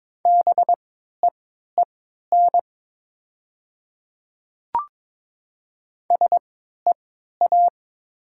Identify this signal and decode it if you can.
Morse code, heard as a keyed tone, about 22 wpm, spaced as for 12 wpm, 690 Hz. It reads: BEEN SEA
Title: 3x Character Spacing - Top 300 Words - 22wpm